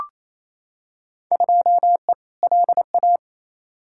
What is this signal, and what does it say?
Morse code, heard as a keyed tone, about 28 wpm, 700 Hz. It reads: 2E LA
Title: Field Day Exchanges Encore - 28wpm